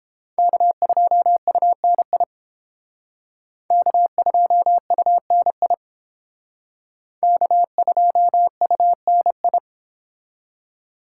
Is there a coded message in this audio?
Yes: Morse code from 0.4 to 9.6 s, heard as a keyed tone, about 31 wpm, 700 Hz. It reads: K2UNI K2UNI K2UNI